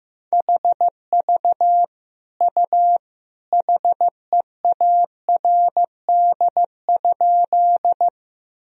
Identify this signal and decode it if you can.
Morse code, heard as a keyed tone, about 14 wpm, 700 Hz. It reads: HV U HEARD?
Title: Sentences from Top 400 Words and Abbreviations 15wpm